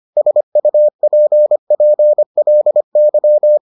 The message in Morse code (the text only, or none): SUPPLY